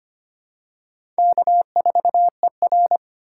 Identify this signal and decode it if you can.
Morse code, heard as a keyed tone, about 25 wpm, 700 Hz. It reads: K4ER
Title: USA Call Signs - 4 Characters - 3x Character Spacing - 25wpm